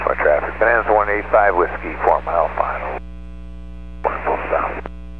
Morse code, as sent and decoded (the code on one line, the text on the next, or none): none